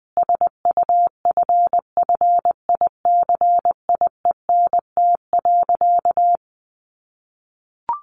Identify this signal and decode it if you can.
Morse code, heard as a keyed tone, about 20 wpm, 700 Hz. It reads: SUFFICIENT.
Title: Sentences from Top 1400 Words 20wpm